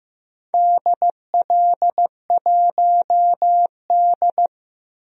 Morse code, heard as a keyed tone, about 16 wpm, 700 Hz. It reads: DL1D
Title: DXCC-100 Call Signs - 4 Characters - 15wpm